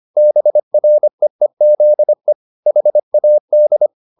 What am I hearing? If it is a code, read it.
Morse code, heard as a keyed tone, about 25 wpm, 600 Hz. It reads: BREEZE HAD